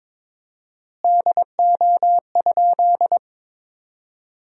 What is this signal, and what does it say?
Morse code, heard as a keyed tone, about 22 wpm, 700 Hz. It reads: DO?
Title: 4x Word-Spacing - Sentences from Top 100 Words - 22wpm